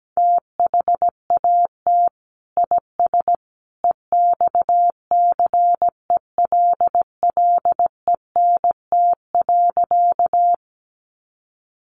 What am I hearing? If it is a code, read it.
Morse code, heard as a keyed tone, about 17 wpm, 700 Hz. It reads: THAT IS EXCELLENT.
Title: Sentences from Top 1000 Words 17wpm